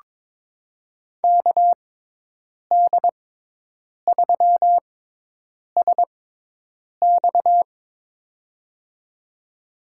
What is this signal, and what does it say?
Morse code, heard as a keyed tone, about 21 wpm, spaced as for 8 wpm, 700 Hz. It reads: KD3SX
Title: USA Call Signs - 5 Characters - 6x Character Spacing - 22wpm